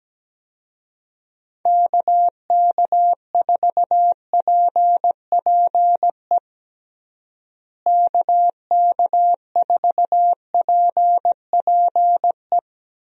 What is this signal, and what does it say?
Morse code, heard as a keyed tone, about 17 wpm, 700 Hz. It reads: KK4PPE KK4PPE
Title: USA Call Signs - 6 Characters - Repeated 3x - 17wpm